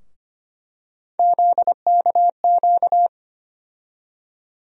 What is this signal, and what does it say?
Morse code, heard as a keyed tone, about 25 wpm, 700 Hz. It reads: ZKQ
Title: USA Call Sign Postfixes - 3 Letters 25wpm